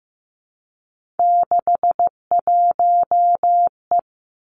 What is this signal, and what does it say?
Morse code, heard as a keyed tone, about 15 wpm, 700 Hz. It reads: 61E
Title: Three Challenging Letter-Number Combinations 15wpm